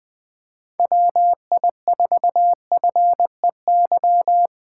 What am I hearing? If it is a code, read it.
Morse code, heard as a keyed tone, about 21 wpm, 700 Hz. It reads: WI4FEY
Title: USA Call Signs - 4x Character Spacing - 20wpm